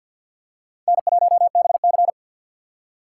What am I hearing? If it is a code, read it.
Morse code, heard as a keyed tone, about 47 wpm, 700 Hz. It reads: N1BC
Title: USA Call Signs - 6x Character Spacing - 50wpm